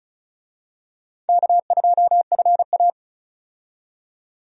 Morse code, heard as a keyed tone, about 34 wpm, 700 Hz. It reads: K2FA
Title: USA Call Signs - 4 Characters - 35wpm